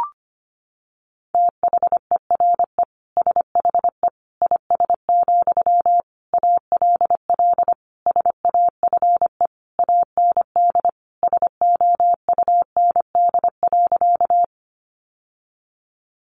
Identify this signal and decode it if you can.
Morse code, heard as a keyed tone, about 25 wpm, 700 Hz. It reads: THERE SHE IS, ALL SAFE AND SOUND.
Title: Sentences from Top 900 Words 25wpm